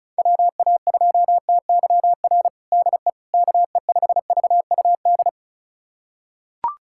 Morse code, heard as a keyed tone, about 34 wpm, 700 Hz. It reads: WA2TYR DE KE5VUD
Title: USA Callsign DE Callsign - 6 Characters - 35wpm